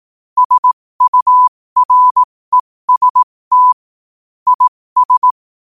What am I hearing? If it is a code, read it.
Morse code, heard as a keyed tone, about 18 wpm, spaced as for 15 wpm, 1000 Hz. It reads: SUREST IS
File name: qotdfiles / 15wpm